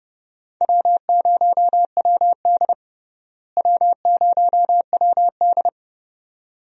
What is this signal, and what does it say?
Morse code, heard as a keyed tone, about 31 wpm, 700 Hz. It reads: W0WD W0WD
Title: USA Call Signs - 4 Characters - Repeated 2x - 30wpm